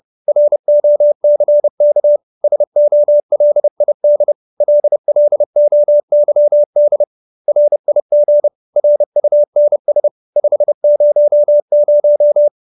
Morse code, heard as a keyed tone, about 30 wpm, 590 Hz. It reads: ROCK SOLID LLOYD RIG RUNS 500